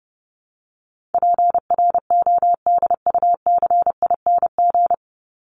Morse code, heard as a keyed tone, about 30 wpm, 700 Hz. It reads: PRODUCING
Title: ING Suffix - Top 100 Words with Base Words 30wpm